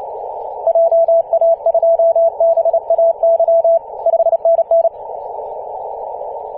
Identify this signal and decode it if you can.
Morse code, heard as a keyed tone, about 30 wpm, 670 Hz. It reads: JA2BAY 5NN
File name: ep2a-21-029mhz-cw